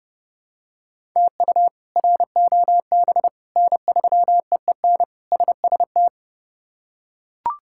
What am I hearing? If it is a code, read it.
Morse code, heard as a keyed tone, about 30 wpm, 700 Hz. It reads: TU ROB N3EEN SST
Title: K1USN SST QSO Practice - 30wpm